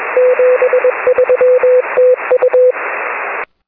Morse code, heard as a keyed tone, about 21 wpm, 500 Hz, with noise. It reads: 73 TU
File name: 6m-n6ml